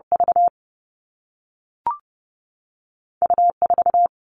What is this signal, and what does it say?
Morse code, heard as a keyed tone, about 30 wpm, spaced as for 3 wpm, 700 Hz. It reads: VU 4
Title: All Cut Numbers - 2-Digit 30wpm